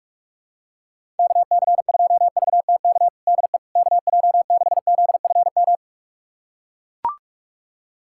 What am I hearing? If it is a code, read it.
Morse code, heard as a keyed tone, about 41 wpm, 700 Hz. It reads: KC2VTK DE KJ6ZFM
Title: USA Callsign DE Callsign 45wpm